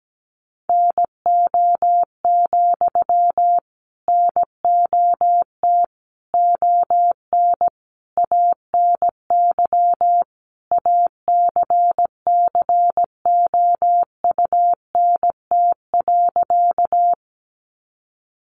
Morse code, heard as a keyed tone, about 17 wpm, 700 Hz. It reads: NO, NOT ON ANY ACCOUNT.
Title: Sentences from Top 900 Words 17wpm